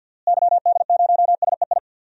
Morse code, heard as a keyed tone, about 53 wpm, 700 Hz. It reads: YD0SEI
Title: DXCC-100 Call Signs - 6 Characters - Repeated 2x - 50wpm